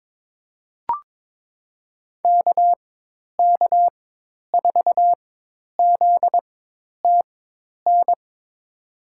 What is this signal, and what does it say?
Morse code, heard as a keyed tone, about 21 wpm, spaced as for 10 wpm, 700 Hz. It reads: KK4ZTN
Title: USA Call Signs - 4x Character Spacing - 22wpm